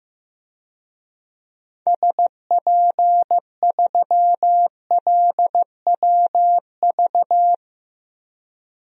Morse code, heard as a keyed tone, about 15 wpm, 700 Hz. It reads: SP3LWV
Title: DXCC-100 Call Signs - 6 Characters - Repeated 2x - 15wpm